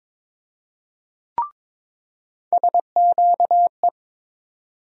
Morse code, heard as a keyed tone, about 21 wpm, 700 Hz. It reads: SQE
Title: USA Call Sign Postfixes - 3 Letters 22wpm